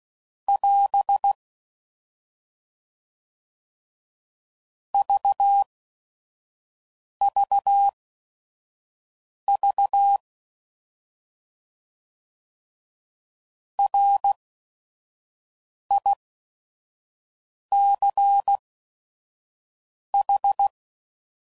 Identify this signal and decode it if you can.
Morse code, heard as a keyed tone, about 16 wpm, spaced as for 5 wpm, 800 Hz. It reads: <AS> VVV RICH
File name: Prefixes / By First Letter / Rs